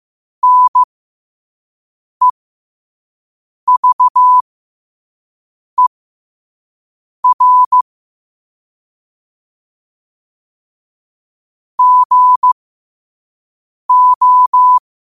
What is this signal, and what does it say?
Morse code, heard as a keyed tone, about 15 wpm, spaced as for 5 wpm, 1000 Hz. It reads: NEVER GO
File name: qotdfiles / 5wpm